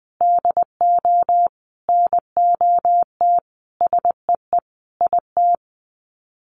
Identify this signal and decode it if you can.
Morse code, heard as a keyed tone, about 20 wpm, 700 Hz. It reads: DO NOT SEE IT
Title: Sets of 5 Words Encore2 20wpm